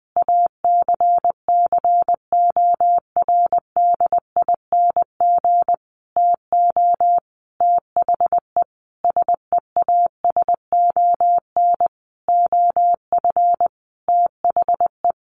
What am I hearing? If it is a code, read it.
Morse code, heard as a keyed tone, about 20 wpm, 700 Hz. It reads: ACCORDING TO THE SEASON OF THE
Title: Sentences from Top 1400 Words 20wpm